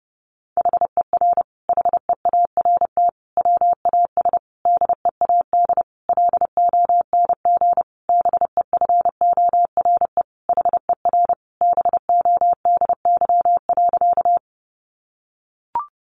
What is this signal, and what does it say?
Morse code, heard as a keyed tone, about 30 wpm, 700 Hz. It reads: HER HEART WAS DEAD LONG BEFORE HER BODY.
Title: Sentences from Top 1500 Words 30wpm